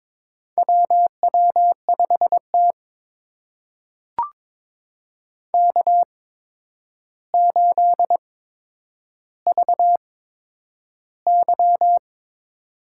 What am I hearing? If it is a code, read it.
Morse code, heard as a keyed tone, about 22 wpm, spaced as for 9 wpm, 700 Hz. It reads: WW5T K8VY
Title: USA Call Signs - 4 Characters - 8x Character Spacing - 22wpm